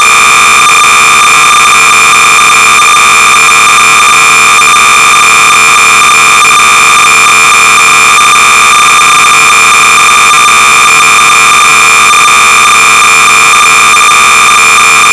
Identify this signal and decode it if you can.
Cassette data (Kansas City type).
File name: sorcerer / 300baud / Maths Tutor-(1980)(W.A.Newton)-BASIC-Sorcerer-300baud